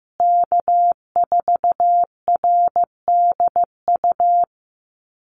Morse code, heard as a keyed tone, about 15 wpm, 700 Hz. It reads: K4RDU
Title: USA Call Signs Repeated 3x 15wpm